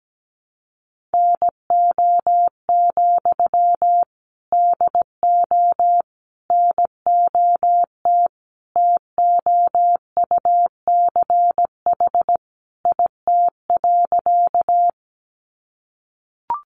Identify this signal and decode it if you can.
Morse code, heard as a keyed tone, about 17 wpm, 700 Hz. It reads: NO, DO NOT TOUCH IT.
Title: Sentences from Top 1200 Words 17wpm